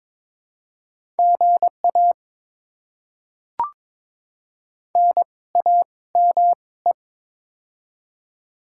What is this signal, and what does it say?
Morse code, heard as a keyed tone, about 22 wpm, spaced as for 17 wpm, 690 Hz. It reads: GA NAME
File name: Top 20 QSO Elements - 2x Character Spacing - 22wpm